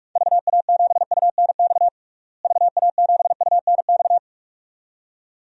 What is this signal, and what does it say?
Morse code, heard as a keyed tone, about 42 wpm, 700 Hz. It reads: VA7UNX VA7UNX
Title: DXCC-100 Call Signs - 6 Characters - Repeated 2x - 45wpm